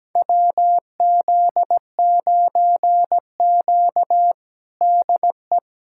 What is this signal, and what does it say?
Morse code, heard as a keyed tone, about 17 wpm, 700 Hz. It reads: WZ9Q DE
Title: USA Callsign DE Callsign 17wpm